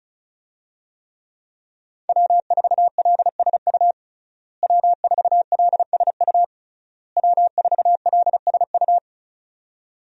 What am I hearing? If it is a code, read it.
Morse code, heard as a keyed tone, about 36 wpm, 700 Hz. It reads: W4LSU W4LSU W4LSU
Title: USA Call Signs - 5 Characters - Repeated 3x - 35wpm